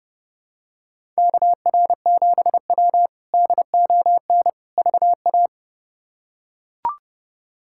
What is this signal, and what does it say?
Morse code, heard as a keyed tone, about 32 wpm, 700 Hz. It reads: KR7W DON VA
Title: K1USN SST QSO Practice - 30wpm